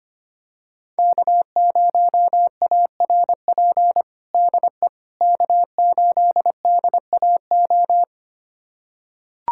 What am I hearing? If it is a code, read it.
Morse code, heard as a keyed tone, about 25 wpm, 700 Hz. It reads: K0ARP DE K8DAO
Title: USA Callsign DE Callsign - 5 Characters - 25wpm